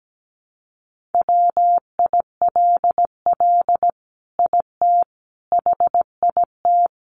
Morse code, heard as a keyed tone, about 17 wpm, 700 Hz. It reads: WILL IT HIT
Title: Sentences from Top 1200 Words 17wpm